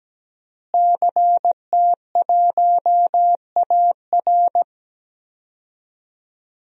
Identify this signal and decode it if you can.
Morse code, heard as a keyed tone, about 17 wpm, 700 Hz. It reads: CT1AR
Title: DXCC-100 Call Signs - 5 Characters - Repeated 3x - 17wpm